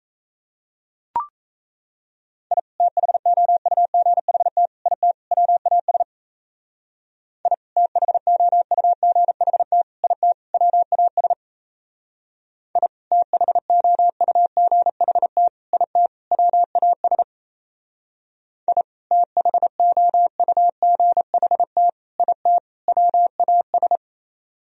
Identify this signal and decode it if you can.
Morse code, heard as a keyed tone, about 33 wpm, 700 Hz. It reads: I THOUGHT IT WAS I THOUGHT IT WAS I THOUGHT IT WAS I THOUGHT IT WAS
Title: Speed-Racing - Sets of 4 Words 28wpm